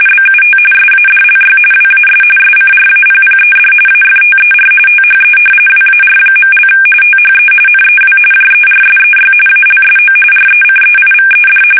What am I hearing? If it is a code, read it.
RTTY, about 45.45 baud. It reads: BUT SADLY WAS SCRAPPED AFTER THE WAR.) THE NEW JERSEY ALSO RECEIVED
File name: fsk850 / battleshipnj